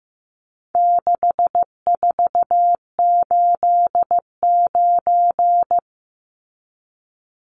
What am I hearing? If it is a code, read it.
Morse code, heard as a keyed tone, about 15 wpm, 700 Hz. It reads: 6489
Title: Numbers - 4-Digit 15wpm